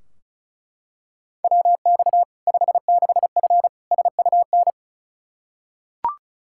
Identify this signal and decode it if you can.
Morse code, heard as a keyed tone, about 36 wpm, 700 Hz. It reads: WX 56F SUN
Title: Practice QSO - Exchange 2 - Info - 35wpm